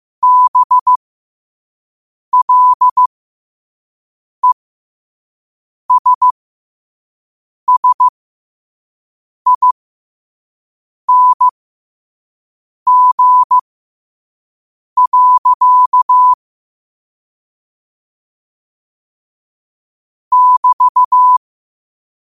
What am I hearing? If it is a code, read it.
Morse code, heard as a keyed tone, about 15 wpm, spaced as for 5 wpm, 1000 Hz. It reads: BLESSING. <BT>
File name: qotdfiles / 5wpm